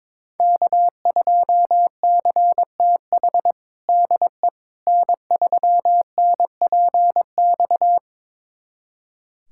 Morse code, heard as a keyed tone, about 21 wpm, 700 Hz. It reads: K2CTH DE N3NPX
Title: USA Callsign DE Callsign - 5 Characters - 22wpm